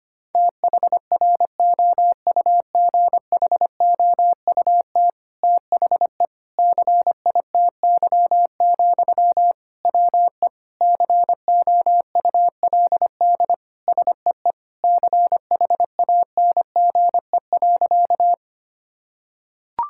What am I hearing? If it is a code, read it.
Morse code, heard as a keyed tone, about 25 wpm, 700 Hz. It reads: THROUGHOUT THE CITY, WE COULD SEE CHANGE.
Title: Sentences from Top 1000 Words - Review Part 5 - 25wpm